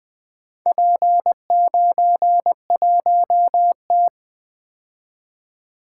Morse code, heard as a keyed tone, about 20 wpm, 700 Hz. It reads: P91T
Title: Four Letter-Number Combinations - 20wpm